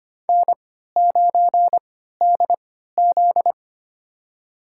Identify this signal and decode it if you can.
Morse code, heard as a keyed tone, about 25 wpm, spaced as for 14 wpm, 700 Hz. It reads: N9DZ
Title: USA Call Signs - 4 Characters - 3x Character Spacing - 25wpm